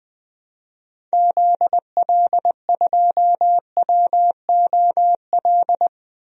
Morse code, heard as a keyed tone, about 21 wpm, 700 Hz. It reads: ZL2WOL
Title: DXCC-100 Call Signs - 6 Characters - 20wpm